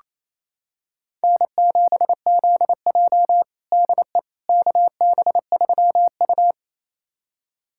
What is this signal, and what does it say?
Morse code, heard as a keyed tone, about 28 wpm, 700 Hz. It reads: N7ZJ DE KB3U
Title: USA Callsign DE Callsign - 4 Characters - 28wpm